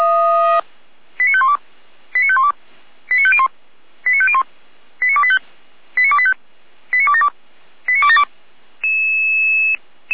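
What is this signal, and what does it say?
Five-tone selective call (ZVEI), heard as quick runs of five tones, tones of about 70 ms. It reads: 89512 89512 89501 89501 89285 89285 89272 89272
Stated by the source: ZVEI five-tone call received on SDR.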